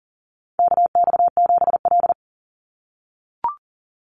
Morse code, heard as a keyed tone, about 40 wpm, 700 Hz. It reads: KX7L
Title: NA SOTA Chaser Callsigns 40wpm